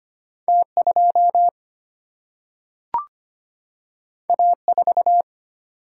Morse code, heard as a keyed tone, about 25 wpm, 700 Hz. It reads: T2 A4
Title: Single-Letter Single-Number 25wpm